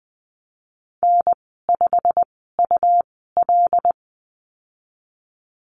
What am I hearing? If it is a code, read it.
Morse code, heard as a keyed tone, about 20 wpm, spaced as for 14 wpm, 700 Hz. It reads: N5UL
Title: 161 Lesson 26g - Letter g - Call Signs 20wpm